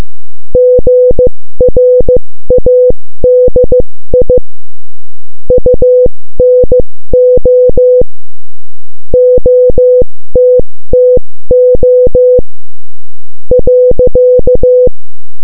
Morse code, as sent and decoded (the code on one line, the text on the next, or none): --. .-. .- -.. .. / ..- -. --- / --- - - --- / .-.-.-
GRADI UNO OTTO .